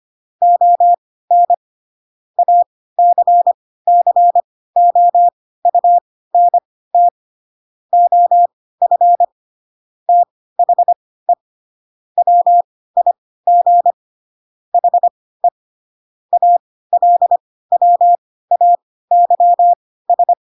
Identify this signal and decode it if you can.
Morse code, heard as a keyed tone, about 25 wpm, spaced as for 16 wpm, 700 Hz. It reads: ON ACCOUNT OF THE WIG HE ALWAYS
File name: Pinocchio / Pinocchio ch2 10 25 ik1hge